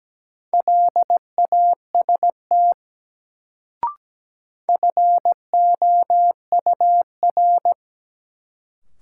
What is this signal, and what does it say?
Morse code, heard as a keyed tone, about 17 wpm, 700 Hz. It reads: LAST FOUR
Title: Sentences from Top 300 Words - New Word Drill - 17wpm